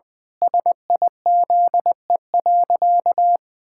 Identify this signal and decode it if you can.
Morse code, heard as a keyed tone, about 20 wpm, 700 Hz. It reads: SIZE.
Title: Sentences from Top 1000 Words - Review Part 6 - 20wpm